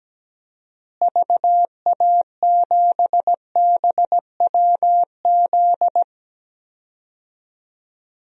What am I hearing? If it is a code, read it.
Morse code, heard as a keyed tone, about 17 wpm, 700 Hz. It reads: VA7BWZ
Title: DXCC-100 Call Signs - 6 Characters - Repeated 2x - 17wpm